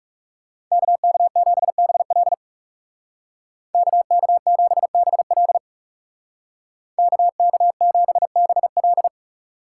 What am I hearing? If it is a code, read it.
Morse code, heard as a keyed tone, about 39 wpm, 700 Hz. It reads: KK7BL KK7BL KK7BL